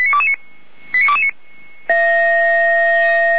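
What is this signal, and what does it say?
Five-tone selective call (ZVEI), heard as quick runs of five tones, tones of about 70 ms. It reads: 89229
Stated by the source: ZVEI five-tone selective call received on SDR, from audio.f85360.feuerwehr-orlamuende.org (Feuerwehr Orlamünde fire brigade).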